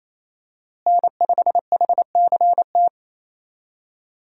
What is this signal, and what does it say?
Morse code, heard as a keyed tone, about 27 wpm, 700 Hz. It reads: N5HCT
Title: USA Call Signs - 5 Characters - Repeated 3x - 28wpm